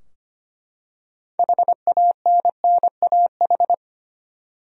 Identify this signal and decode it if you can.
Morse code, heard as a keyed tone, about 25 wpm, 700 Hz. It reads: HANNAH
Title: USA Common Names - 25wpm